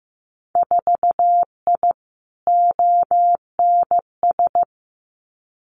Morse code, heard as a keyed tone, about 15 wpm, 700 Hz. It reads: 4I ONS
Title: Winter Field Day Exchanges Encore 15wpm